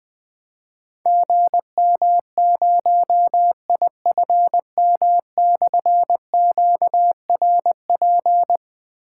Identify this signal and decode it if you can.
Morse code, heard as a keyed tone, about 21 wpm, 700 Hz. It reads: GM0IFM/QRP
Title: Difficult Call Signs 20wpm